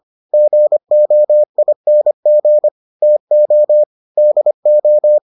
Morse code, heard as a keyed tone, about 25 wpm, 600 Hz. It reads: GOING TO DO